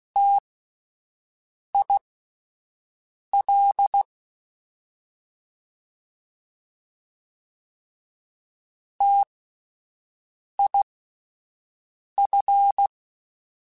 Morse code, heard as a keyed tone, about 16 wpm, spaced as for 5 wpm, 800 Hz. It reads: TIL TIF
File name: Alphabetical / Alphabetical mp3s / Ts @5